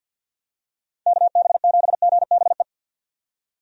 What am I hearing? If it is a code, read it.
Morse code, heard as a keyed tone, about 48 wpm, 700 Hz. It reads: KB7GBE
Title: USA Call Signs - 6 Characters - 50wpm